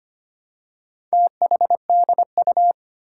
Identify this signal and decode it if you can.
Morse code, heard as a keyed tone, about 25 wpm, 700 Hz. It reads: THDU
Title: Four Challenging Letter-Number Combinations 25wpm